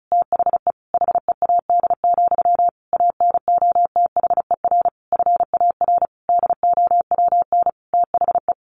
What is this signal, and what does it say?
Morse code, heard as a keyed tone, about 34 wpm, 700 Hz. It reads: THE HEAD, ANOTHER FAR DOWN THE
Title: A Sound Of Thunder 35wpm